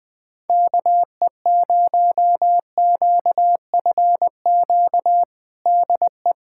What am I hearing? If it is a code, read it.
Morse code, heard as a keyed tone, about 20 wpm, 700 Hz. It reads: KE0QFQ DE
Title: USA Callsign DE Callsign 20wpm